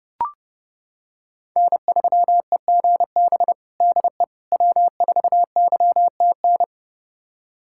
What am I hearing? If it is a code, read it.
Morse code, heard as a keyed tone, about 30 wpm, 700 Hz. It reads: N3EGB DE W4YTN
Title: USA Callsign DE Callsign - 5 Characters - 30wpm